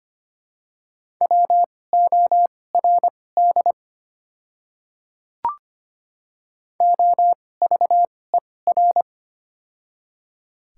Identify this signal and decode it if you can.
Morse code, heard as a keyed tone, about 25 wpm, spaced as for 18 wpm, 700 Hz. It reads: WORD OVER
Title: 2x Character Spacing - Top 100 Words - 25wpm